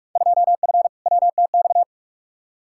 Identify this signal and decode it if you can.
Morse code, heard as a keyed tone, about 46 wpm, 700 Hz. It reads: 2F WTX
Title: Field Day Exchanges - 45wpm